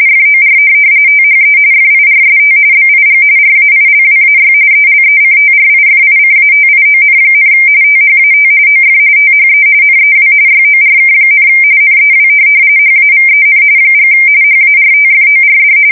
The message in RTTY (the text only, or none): FOR THE FIRST BIG RAILWAY ELECTRIFICATION, THAT OF THE NEW YORK, NEW HAVEN & HARTFORD